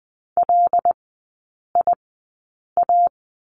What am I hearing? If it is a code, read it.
Morse code, heard as a keyed tone, about 20 wpm, spaced as for 8 wpm, 700 Hz. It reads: LIA
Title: 074 Lesson 13e - Letter L - Three Characters 20wpm